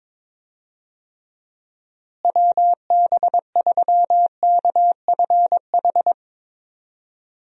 Morse code, heard as a keyed tone, about 23 wpm, 700 Hz. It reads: WB3KFH
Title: USA Call Signs - 6 Characters - Repeated 2x - 22wpm